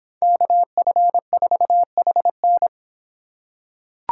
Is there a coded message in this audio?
Yes: Morse code from 0.2 to 2.7 s, heard as a keyed tone, about 26 wpm, 700 Hz. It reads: KF4HN